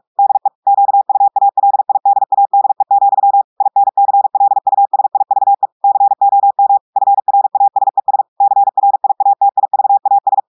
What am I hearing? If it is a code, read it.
Morse code, heard as a keyed tone, about 46 wpm, 800 Hz. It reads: DE QUALIDADE, INCLUSIVE COM FRASES CRIATIVAS
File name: audios / portugais / portugais texte en clair 2 vitesse 45 mots minutes